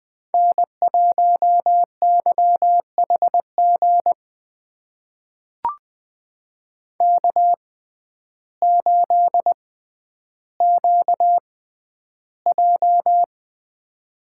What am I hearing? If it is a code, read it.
Morse code, heard as a keyed tone, about 21 wpm, spaced as for 11 wpm, 700 Hz. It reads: N1YHG K8QJ
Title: USA Call Signs - 5 Characters - 6x Character Spacing - 20wpm